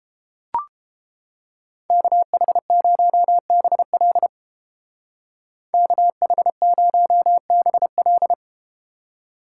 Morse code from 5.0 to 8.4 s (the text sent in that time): KH0BL